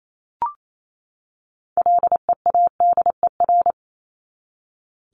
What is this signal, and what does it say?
Morse code, heard as a keyed tone, about 28 wpm, 700 Hz. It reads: LEADER
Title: Top 1501-1750 Words 28wpm